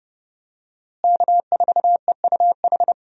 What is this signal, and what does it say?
Morse code, heard as a keyed tone, about 30 wpm, 700 Hz. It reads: K4EUH
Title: USA Call Signs - 5 Characters - Repeated 3x - 30wpm